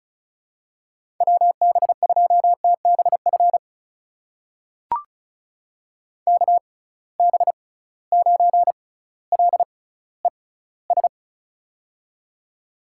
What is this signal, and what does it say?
Morse code, heard as a keyed tone, about 36 wpm, spaced as for 18 wpm, 700 Hz. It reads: WB2TBF KB9LES
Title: USA Call Signs - 6 Characters - 6x Character Spacing - 35wpm